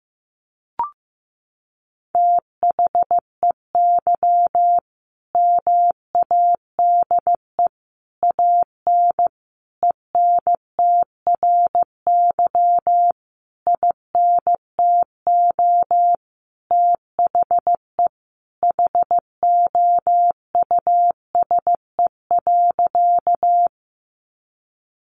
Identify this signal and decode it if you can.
Morse code, heard as a keyed tone, about 15 wpm, 700 Hz. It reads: THEY MADE AN ENTRY INTO THE HOUSE.
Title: Sentences from Top 1500 Words 15wpm